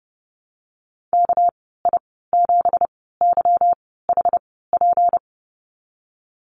Morse code, heard as a keyed tone, about 30 wpm, spaced as for 17 wpm, 700 Hz. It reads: KI7YHP
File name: Lesson 37g - Letter x - Call Signs 30wpm